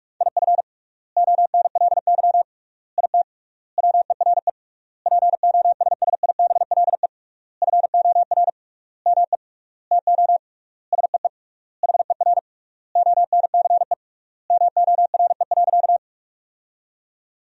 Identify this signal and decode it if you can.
Morse code, heard as a keyed tone, about 41 wpm, 700 Hz. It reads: IF ONLY IT WERE POSSIBLE FOR ME TO SEE HER ONCE MORE.
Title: 3x Word-Spacing - Sentences from Top 500 Words - 45wpm